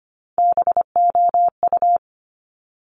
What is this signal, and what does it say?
Morse code, heard as a keyed tone, about 25 wpm, 700 Hz. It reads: BOU
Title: Top Three Letter Combinations in Words 25wpm